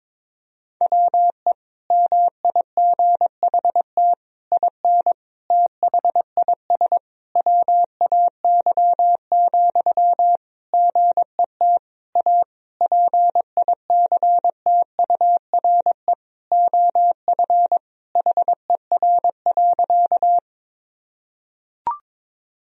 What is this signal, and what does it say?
Morse code, heard as a keyed tone, about 22 wpm, 700 Hz. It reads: WE MIGHT IN THIS WAY, GET A PICTURE OF HER.
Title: Sentences from Top 1000 Words - Review Part 1 - 22wpm